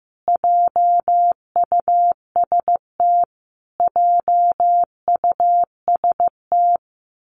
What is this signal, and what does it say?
Morse code, heard as a keyed tone, about 15 wpm, 700 Hz. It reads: JUST JUST
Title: Top 200 Words Repeated 15wpm